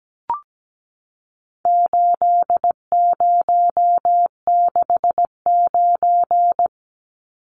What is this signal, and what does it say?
Morse code, heard as a keyed tone, about 17 wpm, 700 Hz. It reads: 8069
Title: Numbers - 4-Digit 17wpm